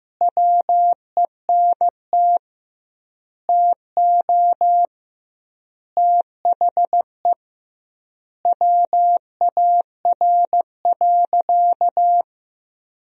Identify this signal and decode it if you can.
Morse code, heard as a keyed tone, about 15 wpm, 700 Hz. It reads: WENT TO THE WAR.
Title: 2x Word-Spacing - Sentences from Top 400 Words - 15wpm